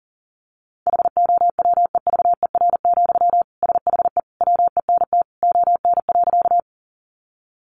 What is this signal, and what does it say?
Morse code, heard as a keyed tone, about 40 wpm, 700 Hz. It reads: HOWEVER, SHE WENT ON.
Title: Sentences from Top 600 Words 40wpm